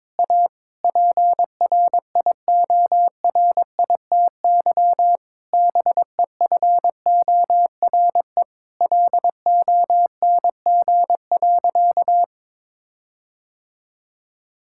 Morse code, heard as a keyed tone, about 22 wpm, 700 Hz. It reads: A PRIORITY BEFORE LONG.
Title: Sentences from Top 1600 Words 22wpm